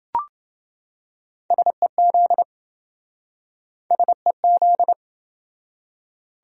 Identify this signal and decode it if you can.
Morse code, heard as a keyed tone, about 27 wpm, 700 Hz. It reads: SEZ SEZ